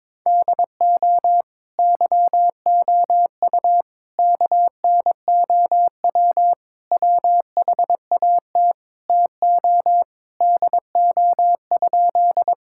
Morse code, heard as a keyed tone, about 22 wpm, 700 Hz. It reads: DO YOU KNOW WHAT TO DO?
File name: practice / 22wpm / Sentences with 2-Word Phrases - 22wpm